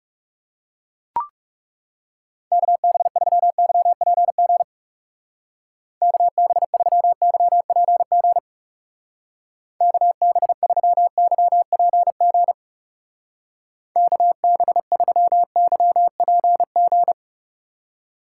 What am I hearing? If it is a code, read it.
Morse code, heard as a keyed tone, about 37 wpm, 700 Hz. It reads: KB3YPG KB3YPG KB3YPG KB3YPG
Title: Speed Racing - USA Call Signs 30wpm